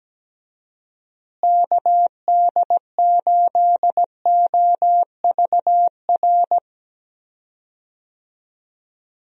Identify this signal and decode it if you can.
Morse code, heard as a keyed tone, about 17 wpm, 700 Hz. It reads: KD8OVR
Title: USA Call Signs - 6 Characters - Repeated 2x - 17wpm